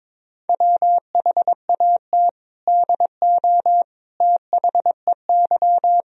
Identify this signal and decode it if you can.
Morse code, heard as a keyed tone, about 22 wpm, 700 Hz. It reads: WHAT DO THEY
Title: Sentences from Top 100 Words and Abbreviations 22wpm